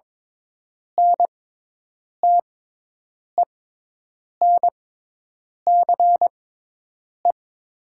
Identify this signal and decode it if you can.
Morse code, heard as a keyed tone, about 22 wpm, spaced as for 8 wpm, 700 Hz. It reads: NTENCE
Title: 6x Character Spacing - Top 500 Words - 22wpm